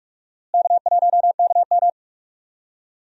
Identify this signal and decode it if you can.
Morse code, heard as a keyed tone, about 47 wpm, 700 Hz. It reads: K1KM
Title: USA Call Signs - 4 Characters - Repeated 2x - 45wpm